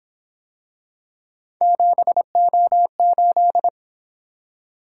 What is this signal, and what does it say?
Morse code, heard as a keyed tone, about 26 wpm, 700 Hz. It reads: 7O8
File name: Speed-Racing - Three Letter-Number Combinations - 22wpm